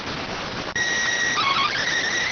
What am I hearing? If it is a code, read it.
SSTV (slow-scan television), PD 290.